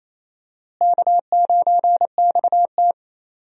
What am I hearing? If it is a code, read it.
Morse code, heard as a keyed tone, about 28 wpm, 700 Hz. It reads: K9XT
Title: USA Call Signs - 4 Characters - Repeated 2x - 28wpm